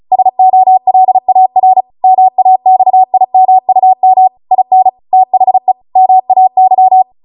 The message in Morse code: SOLAR MAXIMUM IN THE MAY